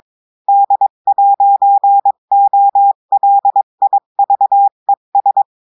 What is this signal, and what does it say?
Morse code, heard as a keyed tone, about 23 wpm, 800 Hz. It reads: D'OLIVES
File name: cours-du-mercredi / audios / cours-853-du-2-mai-2024-etxte chiffre et texte clair-a-20-wpm